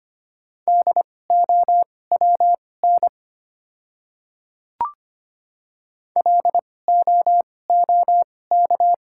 Morse code, heard as a keyed tone, about 25 wpm, spaced as for 18 wpm, 690 Hz. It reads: DOWN LOOK
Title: 2x Character Spacing - Top 100 Words - 25wpm